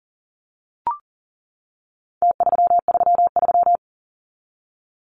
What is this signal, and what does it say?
Morse code, heard as a keyed tone, about 40 wpm, 700 Hz. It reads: T333
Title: Numbers - Contest Style - 4-Digit 40wpm